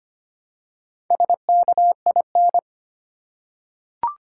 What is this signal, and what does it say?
Morse code, heard as a keyed tone, about 25 wpm, 700 Hz. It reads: SKIN
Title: Four Letter Words - 25wpm